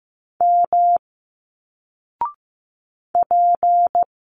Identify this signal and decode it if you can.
Morse code, heard as a keyed tone, about 15 wpm, 700 Hz. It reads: M P
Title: Single Letters 15wpm